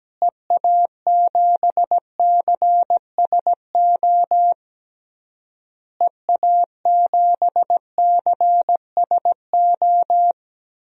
Morse code, heard as a keyed tone, about 17 wpm, 700 Hz. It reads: EA7CSO EA7CSO
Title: DXCC-100 Call Signs - 6 Characters - Repeated 3x - 17wpm